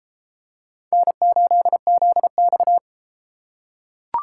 Morse code, heard as a keyed tone, about 33 wpm, 700 Hz. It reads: N8ZX